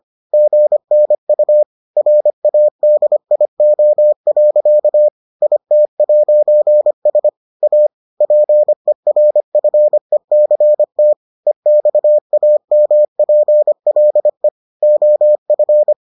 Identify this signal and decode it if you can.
Morse code, heard as a keyed tone, about 25 wpm, 600 Hz. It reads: GNU RADIO. IT'S A PERFECT EXAMPLE OF